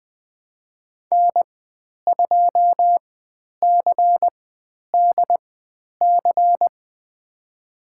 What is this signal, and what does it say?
Morse code, heard as a keyed tone, about 20 wpm, spaced as for 10 wpm, 700 Hz. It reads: N2CDC
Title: 102 Lesson 17g - Letter u - Call Signs 20wpm